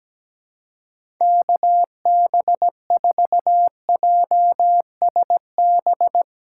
Morse code, heard as a keyed tone, about 17 wpm, 700 Hz. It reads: KB4JSB
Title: USA Call Signs - 6 Characters - 4x Character Spacing - 17wpm